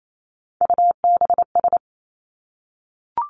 Morse code, heard as a keyed tone, about 28 wpm, 700 Hz. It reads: UBS
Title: Three Challenging Letter-Number Combinations 28wpm